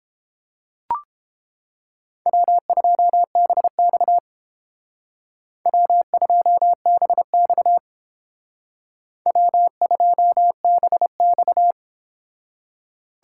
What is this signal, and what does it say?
Morse code, heard as a keyed tone, about 30 wpm, 700 Hz. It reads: W2BX W2BX W2BX